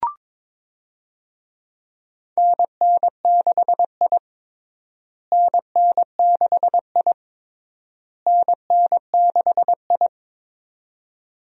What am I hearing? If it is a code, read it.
Morse code, heard as a keyed tone, about 23 wpm, 700 Hz. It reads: NN6I NN6I NN6I
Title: USA Call Signs - 4 Characters - Repeated 3x - 22wpm